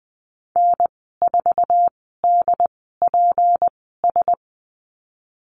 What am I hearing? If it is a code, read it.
Morse code, heard as a keyed tone, about 20 wpm, spaced as for 15 wpm, 700 Hz. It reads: N4DPS
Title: 161 Lesson 26g - Letter g - Call Signs 20wpm